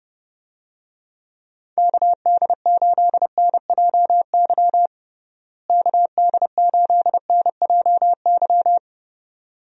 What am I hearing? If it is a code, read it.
Morse code, heard as a keyed tone, about 30 wpm, 700 Hz. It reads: KD8NJY KD8NJY
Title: USA Call Signs - 6 Characters - Repeated 2x - 30wpm